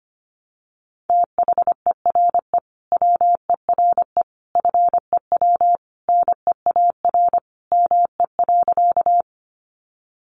Morse code, heard as a keyed tone, about 25 wpm, 700 Hz. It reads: THERE WERE FEW NEAR ME.
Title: Sentences from Top 300 Words 25wpm